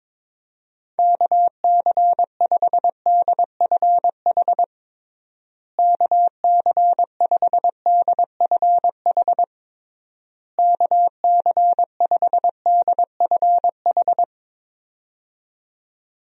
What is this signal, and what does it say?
Morse code, heard as a keyed tone, about 22 wpm, 700 Hz. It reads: KC5DFH KC5DFH KC5DFH
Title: USA Call Signs - 6 Characters - Repeated 3x - 22wpm